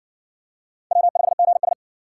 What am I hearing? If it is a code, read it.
Morse code, heard as a keyed tone, about 60 wpm, 700 Hz. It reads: W5GS